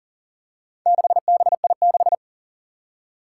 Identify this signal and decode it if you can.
Morse code, heard as a keyed tone, about 40 wpm, 700 Hz. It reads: 6BI6
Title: Four Challenging Letter-Number Combinations 40wpm